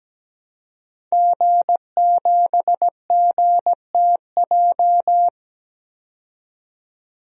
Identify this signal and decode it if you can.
Morse code, heard as a keyed tone, about 17 wpm, 700 Hz. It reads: G7GTJ
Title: DXCC-100 Call Signs - 5 Characters - 17wpm